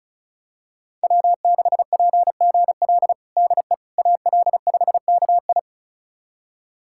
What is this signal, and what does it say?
Morse code, heard as a keyed tone, about 34 wpm, 700 Hz. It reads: W6PGL DE AL5KI
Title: USA Callsign DE Callsign - 5 Characters - 35wpm